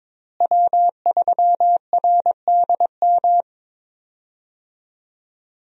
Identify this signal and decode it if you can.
Morse code, heard as a keyed tone, about 22 wpm, 700 Hz. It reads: W3RDM
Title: USA Call Signs - 5 Characters - Repeated 2x - 22wpm